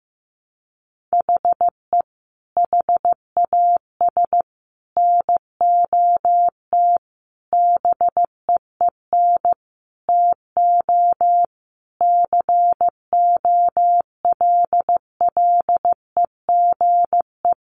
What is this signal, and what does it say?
Morse code, heard as a keyed tone, about 15 wpm, 700 Hz. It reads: HE HAS NOT BEEN TO COLLEGE
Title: Sentences from Top 1300 Words 15wpm